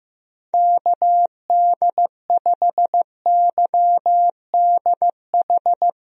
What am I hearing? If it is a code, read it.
Morse code, heard as a keyed tone, about 15 wpm, 700 Hz. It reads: KD5YDH
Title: USA Call Signs - 6 Characters - 15wpm